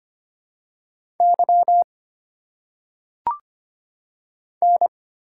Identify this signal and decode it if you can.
Morse code, heard as a keyed tone, about 25 wpm, 700 Hz. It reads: Y N
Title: Single Letters - Phonetic Pronunciation 25wpm